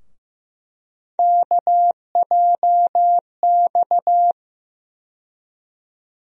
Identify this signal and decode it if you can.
Morse code, heard as a keyed tone, about 15 wpm, 700 Hz. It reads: KJX
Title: USA Call Sign Postfixes - 3 Letters 15wpm